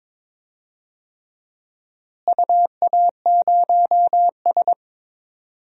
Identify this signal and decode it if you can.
Morse code, heard as a keyed tone, about 22 wpm, 700 Hz. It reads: UA0S
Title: DXCC-100 Call Signs - 4 Characters - Repeated 3x - 22wpm